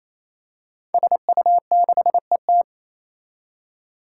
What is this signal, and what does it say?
Morse code, heard as a keyed tone, about 27 wpm, 700 Hz. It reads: SU6ET
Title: Five Challenging Letter-Number Combinations 28wpm